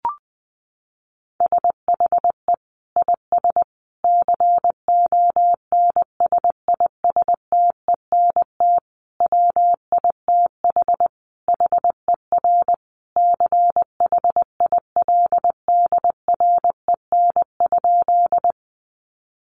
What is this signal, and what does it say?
Morse code, heard as a keyed tone, about 20 wpm, 700 Hz. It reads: SHE IS CONSISTENT WITH HER CHILDREN?
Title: Sentences from Top 1500 Words 20wpm